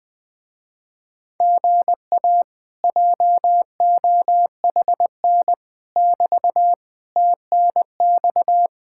Morse code, heard as a keyed tone, about 20 wpm, 700 Hz. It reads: GA JOHN <BT> TNX